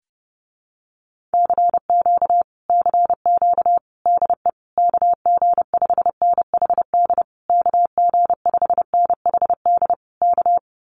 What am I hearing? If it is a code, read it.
Morse code, heard as a keyed tone, about 30 wpm, 700 Hz. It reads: CQ CQ DE KG5NHD KG5NHD K
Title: Common QSO Phrases Encore 30wpm